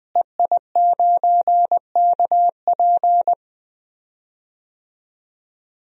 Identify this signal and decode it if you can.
Morse code, heard as a keyed tone, about 20 wpm, 700 Hz. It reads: EI9KP
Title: DXCC-100 Call Signs - 5 Characters - Repeated 2x - 20wpm